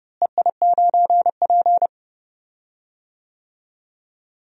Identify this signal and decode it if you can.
Morse code, heard as a keyed tone, about 30 wpm, 700 Hz. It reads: EI9P
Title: DXCC-100 Call Signs - 4 Characters - Repeated 3x - 30wpm